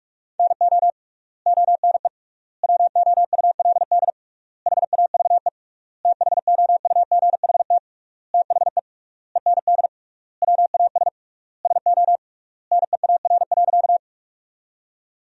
Morse code, heard as a keyed tone, about 46 wpm, 700 Hz. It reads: NO ONE WOULD HAVE THOUGHT THE END WAS SO NEAR.
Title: 3x Word-Spacing - Sentences from Top 300 Words - 45wpm